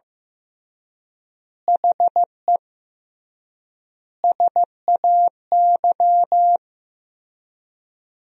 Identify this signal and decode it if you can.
Morse code, heard as a keyed tone, about 16 wpm, 700 Hz. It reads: HE SAY
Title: 3x Word-Spacing - Sentences from Top 300 Words - 15wpm